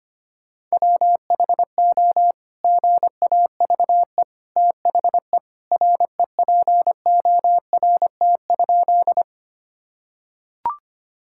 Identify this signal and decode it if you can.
Morse code, heard as a keyed tone, about 25 wpm, 700 Hz. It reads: WHO GAVE THE REPORT?
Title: Sentences from Top 1000 Words - Review Part 6 - 25wpm